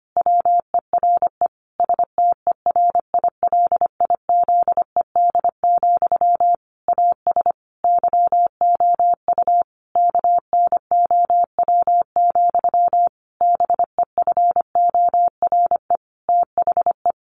Morse code, heard as a keyed tone, about 25 wpm, 700 Hz. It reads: WERE STERILIZED, AS YOU KNOW, BEFORE THE
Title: A Sound Of Thunder 25wpm